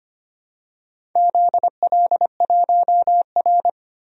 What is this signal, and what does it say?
Morse code, heard as a keyed tone, about 25 wpm, 700 Hz. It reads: ZL1R
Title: DXCC-100 Call Signs - 4 Characters - 25wpm